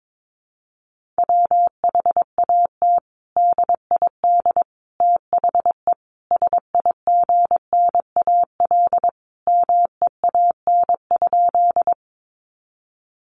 Morse code, heard as a keyed tone, about 22 wpm, 700 Hz. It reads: WHAT DID THE SIGNAL MEAN?
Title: Sentences from Top 1400 Words 22wpm